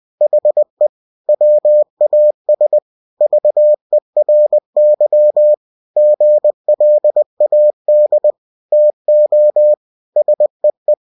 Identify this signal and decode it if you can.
Morse code, heard as a keyed tone, about 20 wpm, 600 Hz. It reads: HE WAS VERY GLAD TO SEE
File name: cw-books / winnie-the-pooh / winnie-20wpm-page-0015